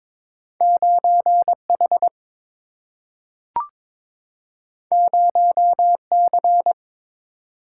Morse code, heard as a keyed tone, about 22 wpm, 700 Hz. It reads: 9H 0C
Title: Single-Letter Single-Number 22wpm